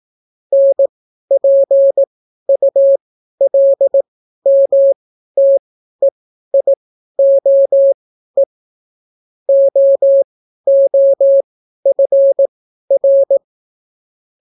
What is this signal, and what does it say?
Morse code, heard as a keyed tone, about 18 wpm, spaced as for 12 wpm, 550 Hz. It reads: NPULMTEIOE OOFR